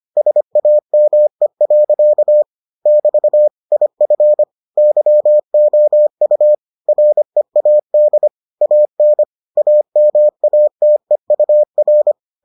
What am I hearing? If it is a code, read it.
Morse code, heard as a keyed tone, about 25 wpm, 600 Hz. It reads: SAME. <BT> IF YOU READ AN AMATEUR